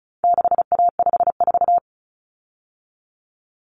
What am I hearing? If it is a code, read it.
Morse code, heard as a keyed tone, about 34 wpm, 700 Hz. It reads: 6A54
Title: Common Cut Numbers - 4-Digit 35wpm